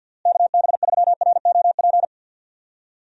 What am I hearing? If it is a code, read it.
Morse code, heard as a keyed tone, about 52 wpm, 700 Hz. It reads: KB3ROP
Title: USA Call Signs - 6 Characters - Repeated 3x - 50wpm